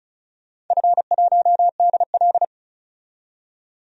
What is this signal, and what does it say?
Morse code, heard as a keyed tone, about 36 wpm, 700 Hz. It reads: F1DL
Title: DXCC-100 Call Signs - 4 Characters - 35wpm